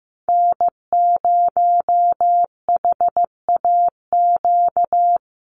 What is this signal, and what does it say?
Morse code, heard as a keyed tone, about 15 wpm, 700 Hz. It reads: N0HAQ
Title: USA Call Signs 15wpm